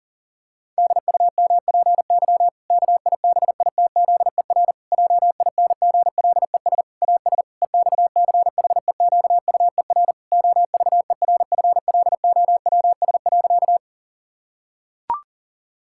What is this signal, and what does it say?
Morse code, heard as a keyed tone, about 40 wpm, 700 Hz. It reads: DUMPY KIBITZER JINGLES AS EXCHEQUER OVERFLOWS.
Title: Pangram Challenge v2 - 40wpm